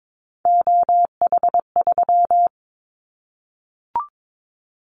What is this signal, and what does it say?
Morse code, heard as a keyed tone, about 22 wpm, 700 Hz. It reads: OH3
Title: Callsign Prefixes for Top 25 DXCC - Repeated 3x - 22wpm